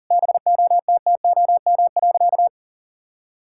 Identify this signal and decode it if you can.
Morse code, heard as a keyed tone, about 37 wpm, 700 Hz. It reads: BOTTOM.
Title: Sentences from Top 1000 Words - Review Part 6 - 40wpm